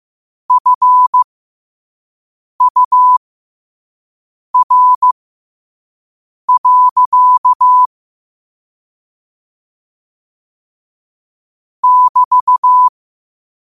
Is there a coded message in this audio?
Yes: Morse code from 0.5 to 12.9 s, heard as a keyed tone, about 15 wpm, spaced as for 5 wpm, 1000 Hz. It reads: FUR. <BT>